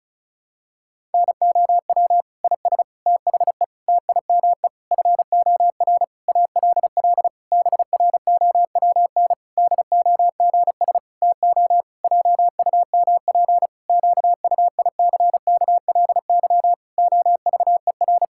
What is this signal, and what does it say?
Morse code, heard as a keyed tone, about 36 wpm, 700 Hz. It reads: NOW IS THE TIME FOR ALL BROWN DOGS TO JUMP QUICKLY OVER
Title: Pangram Challenge v2 - 35wpm